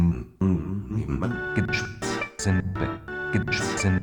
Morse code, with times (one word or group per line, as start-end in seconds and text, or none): none